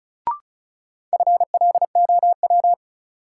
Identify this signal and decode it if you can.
Morse code, heard as a keyed tone, about 36 wpm, 700 Hz. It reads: FLOW
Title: Sentences from Top 1200 Words - New Word Drill Alt - 35wpm